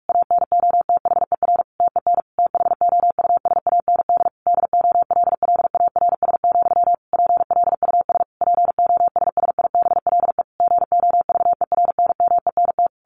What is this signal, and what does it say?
Morse code, heard as a keyed tone, about 44 wpm, 700 Hz. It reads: ANOTHER TEN THOUSAND DOLLARS, PLUS POSSIBLE GOVERNMENT
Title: A Sound Of Thunder 45wpm